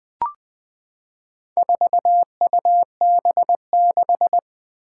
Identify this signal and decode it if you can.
Morse code, heard as a keyed tone, about 20 wpm, 700 Hz. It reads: 4UB6
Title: Four Challenging Letter-Number Combinations 20wpm